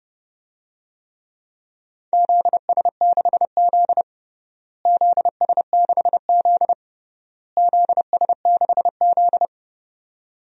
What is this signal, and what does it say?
Morse code, heard as a keyed tone, about 29 wpm, 700 Hz. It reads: ZS6Z ZS6Z ZS6Z
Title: DXCC-100 Call Signs - 4 Characters - Repeated 3x - 30wpm